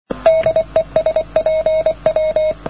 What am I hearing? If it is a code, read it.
Morse code, heard as a keyed tone, about 28 wpm, 590 Hz. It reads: DE SPW
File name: europe / pol / pol-252-spw-cw-only-andystumpf